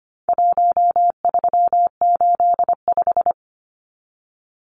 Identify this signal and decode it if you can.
Morse code, heard as a keyed tone, about 25 wpm, 700 Hz. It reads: 1385
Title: Numbers - 4-Digit 25wpm